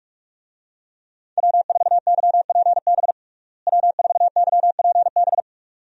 Morse code, heard as a keyed tone, about 41 wpm, 700 Hz. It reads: W4YPB W4YPB
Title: USA Call Signs - 5 Characters - Repeated 2x - 45wpm